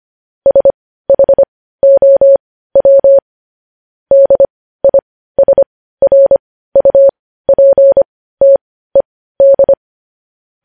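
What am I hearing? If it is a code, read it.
Morse code, heard as a keyed tone, about 25 wpm, spaced as for 15 wpm, 550 Hz. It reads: SHOW DISRUPTED
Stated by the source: Morse practice sending news headlines at 15 wpm.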